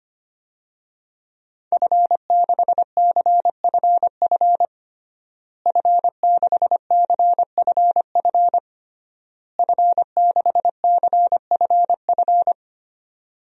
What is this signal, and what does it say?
Morse code, heard as a keyed tone, about 25 wpm, 700 Hz. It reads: F6CFF F6CFF F6CFF
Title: DXCC-100 Call Signs - 5 Characters - Repeated 3x - 25wpm